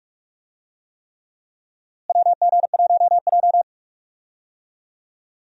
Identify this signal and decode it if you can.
Morse code, heard as a keyed tone, about 46 wpm, 700 Hz. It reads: WG1J
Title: USA Call Signs - 4 Characters - Repeated 3x - 45wpm